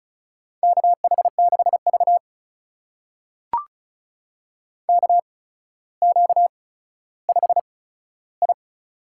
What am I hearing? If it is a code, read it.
Morse code, heard as a keyed tone, about 36 wpm, spaced as for 15 wpm, 700 Hz. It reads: KH6V KQ5I
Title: USA Call Signs - 8x Character Spacing - 35wpm